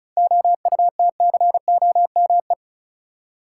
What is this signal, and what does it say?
Morse code, heard as a keyed tone, about 34 wpm, 700 Hz. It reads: OUTCOME
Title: Seven Letter Words - 35wpm